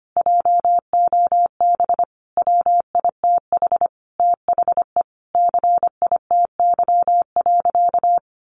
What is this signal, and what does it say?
Morse code, heard as a keyed tone, about 25 wpm, 700 Hz. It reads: JOB WITH THE CITY.
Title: Sentences from Top 1500 Words 25wpm